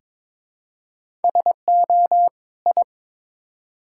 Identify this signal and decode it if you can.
Morse code, heard as a keyed tone, about 22 wpm, 700 Hz. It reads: SO I
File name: practice / 22wpm / Build A Phrase - 5-Word Phrases - 22wpm